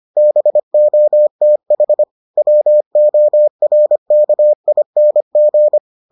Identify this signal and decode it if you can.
Morse code, heard as a keyed tone, about 25 wpm, 600 Hz. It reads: BOTH WORKING